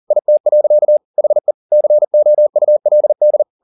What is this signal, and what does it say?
Morse code, heard as a keyed tone, about 40 wpm, 600 Hz. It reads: IT. HE COULD
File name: the old man and the sea 2 40wpm